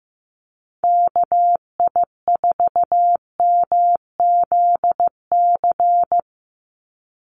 Morse code, heard as a keyed tone, about 15 wpm, 700 Hz. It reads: KI4MZC
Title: Call Signs Repeated 2x - 15wpm